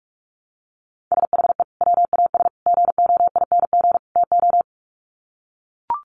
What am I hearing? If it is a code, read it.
Morse code, heard as a keyed tone, about 44 wpm, 700 Hz. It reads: SHE WAS GOING TO
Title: Sets of 4 Words 45wpm